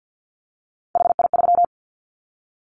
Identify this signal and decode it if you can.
Morse code, heard as a keyed tone, about 50 wpm, 700 Hz. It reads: HI3
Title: Callsign Prefixes for Top 100 DXCC - 50wpm